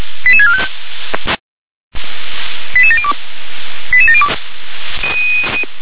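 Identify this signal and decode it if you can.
Five-tone selective call (ZVEI), heard as quick runs of five tones, tones of about 70 ms. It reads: 88645 88702 88702
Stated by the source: Received by SDR as ZVEI five-tone selective call.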